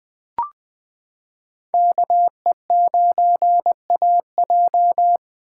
Morse code, heard as a keyed tone, about 21 wpm, 700 Hz. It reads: KE9AJ
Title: 2023 SOTA Chaser Call Signs - 20wpm